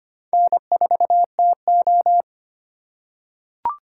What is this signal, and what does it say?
Morse code, heard as a keyed tone, about 25 wpm, 700 Hz. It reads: N4TO
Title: USA Call Signs - 4 Characters - 2x Character Spacing - 25wpm